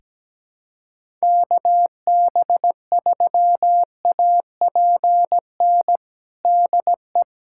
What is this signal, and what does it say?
Morse code, heard as a keyed tone, about 17 wpm, 700 Hz. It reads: KB3APN DE
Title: USA Callsign DE Callsign - 6 Characters - 17wpm